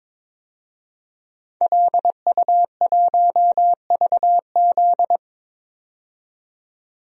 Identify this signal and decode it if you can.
Morse code, heard as a keyed tone, about 22 wpm, 700 Hz. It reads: LU1VZ
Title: DXCC-100 Call Signs - 5 Characters - 22wpm